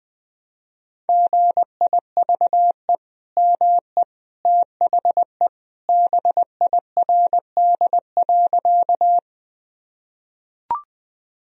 Morse code, heard as a keyed tone, about 21 wpm, 700 Hz. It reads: GIVE ME THE BIRD.
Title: Sentences from Top 1000 Words - Review Part 2 - 20wpm